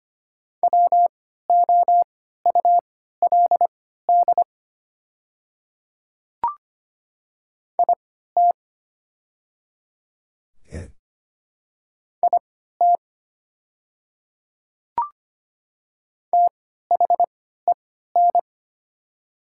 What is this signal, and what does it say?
Morse code, heard as a keyed tone, about 25 wpm, spaced as for 14 wpm, 700 Hz. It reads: WOULD IT IT THEN
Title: 3x Character Spacing - Top 100 Words - 25wpm